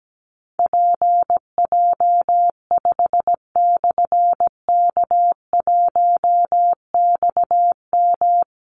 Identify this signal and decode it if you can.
Morse code, heard as a keyed tone, about 17 wpm, 700 Hz. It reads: PJ5/K1XM
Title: Call Signs Repeated 3x - 17wpm